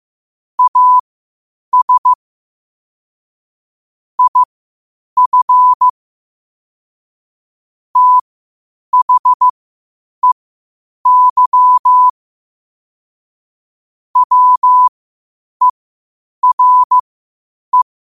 Morse code, heard as a keyed tone, about 15 wpm, spaced as for 8 wpm, 1000 Hz. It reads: AS IF THEY WERE
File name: qotdfiles / 8wpm